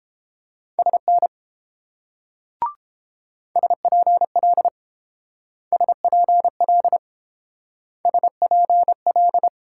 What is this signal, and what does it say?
Morse code, heard as a keyed tone, about 28 wpm, 690 Hz. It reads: SN SPL SPL SPL